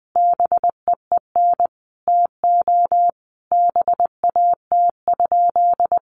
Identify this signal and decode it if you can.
Morse code, heard as a keyed tone, about 19 wpm, 700 Hz. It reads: BEEN TO BAT?
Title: Sentences from Top 1500 Words 20wpm